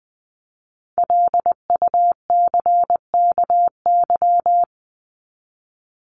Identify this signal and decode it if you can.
Morse code, heard as a keyed tone, about 20 wpm, 700 Hz. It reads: LUCKY
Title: Top 1751-2000 Words 20wpm